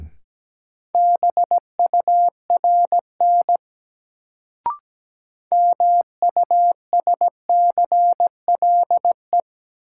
Morse code, heard as a keyed tone, about 17 wpm, 700 Hz. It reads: BURN MUSCLE
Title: Sentences from Top 1300 Words - New Word Drill Alt - 17wpm